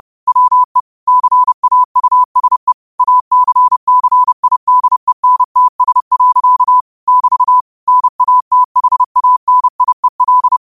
Morse code, heard as a keyed tone, about 30 wpm, 1000 Hz. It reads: WE CAUSE ACCIDENTS. <BT> NATHANIEL
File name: qotdfiles / 30wpm